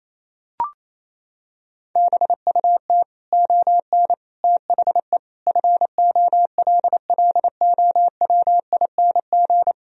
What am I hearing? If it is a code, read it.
Morse code, heard as a keyed tone, about 28 wpm, 700 Hz. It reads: BUT ON THE FOLLOWING
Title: Sentences from Top 1000 Words - Review Part 4 - 28wpm